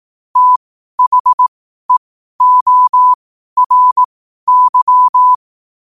Morse code, heard as a keyed tone, about 18 wpm, spaced as for 12 wpm, 1000 Hz. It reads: THEORY
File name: qotdfiles / 12wpm